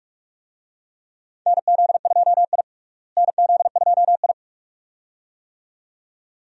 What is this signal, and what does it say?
Morse code, heard as a keyed tone, about 46 wpm, 700 Hz. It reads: NZ2I NZ2I
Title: USA Call Signs - 4 Characters - Repeated 2x - 45wpm